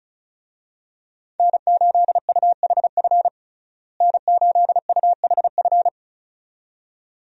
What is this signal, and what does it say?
Morse code, heard as a keyed tone, about 36 wpm, 700 Hz. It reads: N8UHF N8UHF
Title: USA Call Signs - 5 Characters - Repeated 2x - 35wpm